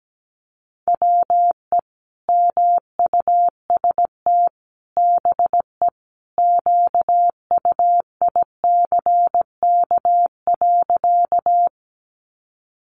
Morse code, heard as a keyed tone, about 17 wpm, 700 Hz. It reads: WE MUST BE QUICK.
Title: Sentences from Top 500 Words 17wpm